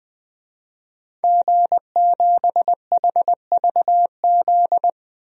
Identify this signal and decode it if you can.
Morse code, heard as a keyed tone, about 20 wpm, 700 Hz. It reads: G7HVZ
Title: DXCC-100 Call Signs - 5 Characters - 20wpm